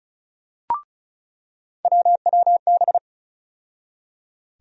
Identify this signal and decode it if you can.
Morse code, heard as a keyed tone, about 36 wpm, 700 Hz. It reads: WWB
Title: USA Call Sign Postfixes - 3 Letters 35wpm